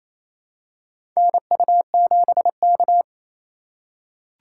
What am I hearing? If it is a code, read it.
Morse code, heard as a keyed tone, about 28 wpm, 700 Hz. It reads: NU7K
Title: USA Call Signs - 4 Characters - 4x Character Spacing - 28wpm